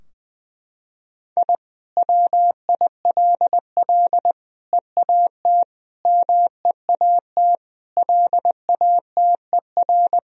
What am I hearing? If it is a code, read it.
Morse code, heard as a keyed tone, about 20 wpm, 700 Hz. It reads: I WILL EAT MEAT LATER
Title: Sentences from Top 1000 Words - Review Part 3 - 20wpm